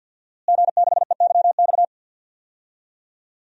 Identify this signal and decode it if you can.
Morse code, heard as a keyed tone, about 52 wpm, 700 Hz. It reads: G6EYX
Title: DXCC-100 Call Signs - 5 Characters - Repeated 3x - 50wpm